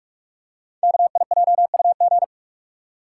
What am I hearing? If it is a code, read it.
Morse code, heard as a keyed tone, about 44 wpm, 690 Hz. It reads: KIJUG
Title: Code Groups - 5 Letters - 45wpm